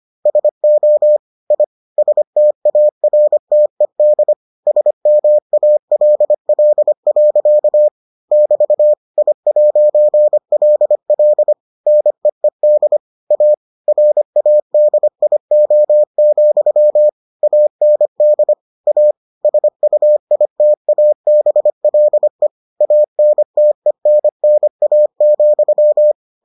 Morse code, heard as a keyed tone, about 25 wpm, 600 Hz. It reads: SO I STARTED SMALL. <BT> I'LL NEED A RADIO, AND A SUITABLE ANTENNA,